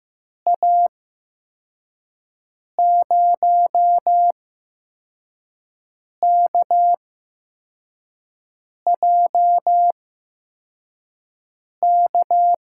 Morse code, heard as a keyed tone, about 15 wpm, spaced as for 4 wpm, 700 Hz. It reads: A0KJK
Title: USA Call Signs - 6 Characters - 8x Character Spacing - 15wpm